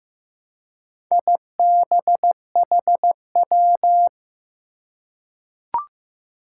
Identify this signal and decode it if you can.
Morse code, heard as a keyed tone, about 15 wpm, 700 Hz. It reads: IBHW
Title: Four Challenging Letter-Number Combinations 15wpm